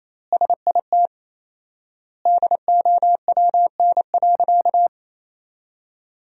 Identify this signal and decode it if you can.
Morse code, heard as a keyed tone, about 28 wpm, 700 Hz. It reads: SIT DOWN.
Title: 4x Word-Spacing - Sentences from Top 500 Words - 28wpm